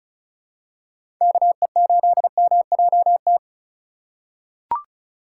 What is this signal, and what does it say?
Morse code, heard as a keyed tone, about 36 wpm, 700 Hz. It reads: KE8MJT
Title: USA Call Signs - 6 Characters - 3x Character Spacing - 35wpm